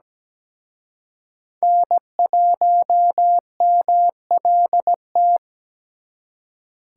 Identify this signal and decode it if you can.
Morse code, heard as a keyed tone, about 20 wpm, 700 Hz. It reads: N1MLT
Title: Speed Racing - USA 1x3 Call Signs 17wpm